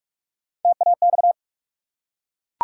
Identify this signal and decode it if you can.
Morse code, heard as a keyed tone, about 44 wpm, 700 Hz. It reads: TAX
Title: Three Letter Words - 45wpm